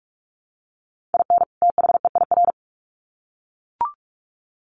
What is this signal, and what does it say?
Morse code, heard as a keyed tone, about 44 wpm, 700 Hz. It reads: IN THEIR
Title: Sets of 2 Words 45wpm